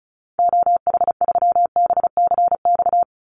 Morse code, heard as a keyed tone, about 34 wpm, 700 Hz. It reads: OH3BCX
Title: Call Signs Repeated 2x - 35wpm